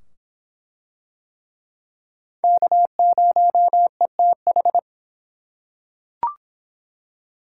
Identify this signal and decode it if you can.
Morse code, heard as a keyed tone, about 23 wpm, 700 Hz. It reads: K0ETH